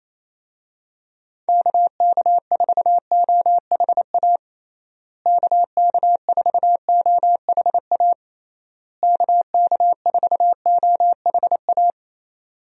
Morse code, heard as a keyed tone, about 27 wpm, 700 Hz. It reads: KK4OHA KK4OHA KK4OHA
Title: USA Call Signs - 6 Characters - Repeated 3x - 28wpm